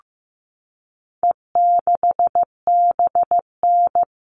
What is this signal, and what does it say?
Morse code, heard as a keyed tone, about 15 wpm, 700 Hz. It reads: E6BN
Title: All Cut Numbers - 4-Digit 15wpm